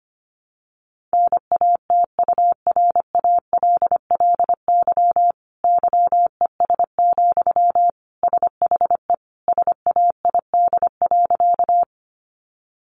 Morse code, heard as a keyed tone, about 25 wpm, 700 Hz. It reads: NATURALLY YES, SHE SAID.
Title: Sentences from Top 1300 Words 25wpm